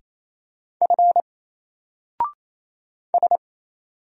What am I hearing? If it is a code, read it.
Morse code, heard as a keyed tone, about 27 wpm, 700 Hz. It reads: F S
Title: Single Letter-Number Alt 28wpm